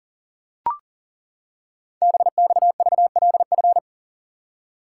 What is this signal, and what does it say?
Morse code, heard as a keyed tone, about 40 wpm, 700 Hz. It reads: BXVLF
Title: Code Groups - 5 Letters - 40wpm